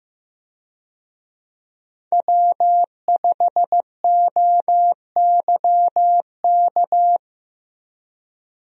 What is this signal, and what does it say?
Morse code, heard as a keyed tone, about 15 wpm, 700 Hz. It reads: W5OYK
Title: USA Call Signs - 5 Characters - 15wpm